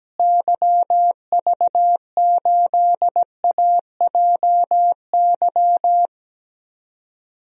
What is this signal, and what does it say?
Morse code, heard as a keyed tone, about 17 wpm, 700 Hz. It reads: YV8AJY
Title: DXCC-100 Call Signs - 6 Characters - Repeated 3x - 17wpm